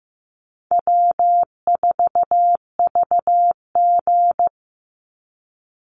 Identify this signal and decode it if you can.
Morse code, heard as a keyed tone, about 15 wpm, 700 Hz. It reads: W4VG
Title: Easy Call Signs Repeated 2x - 15wpm